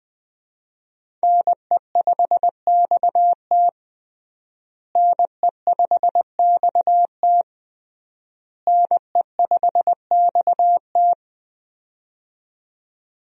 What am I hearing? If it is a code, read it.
Morse code, heard as a keyed tone, about 20 wpm, 700 Hz. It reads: NE5XT NE5XT NE5XT
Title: USA Call Signs - 5 Characters - Repeated 3x - 20wpm